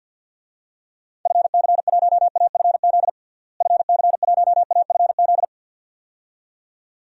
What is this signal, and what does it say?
Morse code, heard as a keyed tone, about 46 wpm, 700 Hz. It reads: FC1AFZ FC1AFZ
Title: DXCC-100 Call Signs - 6 Characters - Repeated 2x - 50wpm